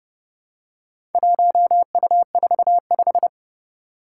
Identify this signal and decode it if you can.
Morse code, heard as a keyed tone, about 31 wpm, 700 Hz. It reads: 1U45
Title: Four Challenging Letter-Number Combinations 30wpm